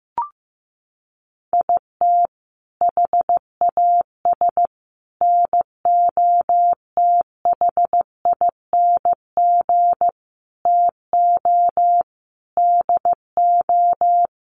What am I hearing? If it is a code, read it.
Morse code, heard as a keyed tone, about 15 wpm, 700 Hz. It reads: IT HAS NOTHING TO DO
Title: Sets of 5 Words Encore 15wpm